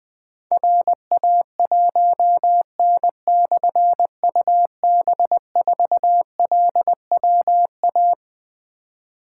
Morse code, heard as a keyed tone, about 20 wpm, 700 Hz. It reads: RA1N/UB4LWA
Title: Difficult Call Signs 20wpm